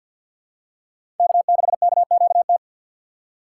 Morse code, heard as a keyed tone, about 52 wpm, 700 Hz. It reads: K6KQT
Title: USA Call Signs - 5 Characters - 3x Character Spacing - 50wpm